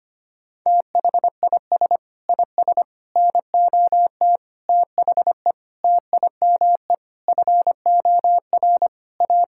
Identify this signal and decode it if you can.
Morse code, heard as a keyed tone, about 25 wpm, 700 Hz. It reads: THIS IS NOT THE TIME FOR A
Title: Sentences from Top 1000 Words - Review Part 5 - 25wpm